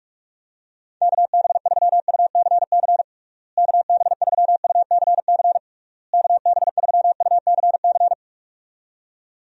Morse code, heard as a keyed tone, about 46 wpm, 700 Hz. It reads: KB3UCC KB3UCC KB3UCC
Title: USA Call Signs - 6 Characters - Repeated 3x - 45wpm